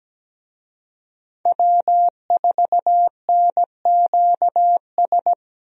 Morse code, heard as a keyed tone, about 17 wpm, 700 Hz. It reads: W4NQS
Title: USA Call Signs - 5 Characters - Repeated 2x - 17wpm